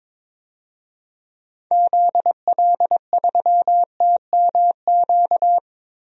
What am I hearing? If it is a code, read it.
Morse code, heard as a keyed tone, about 21 wpm, 700 Hz. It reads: ZL3TMQ
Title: DXCC-100 Call Signs - 6 Characters - 22wpm